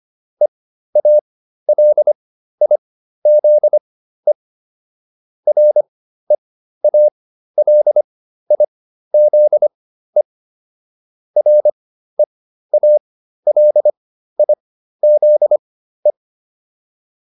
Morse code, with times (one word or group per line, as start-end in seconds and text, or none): 0.4-4.3 EALIZE
5.5-10.2 REALIZE
11.4-16.1 REALIZE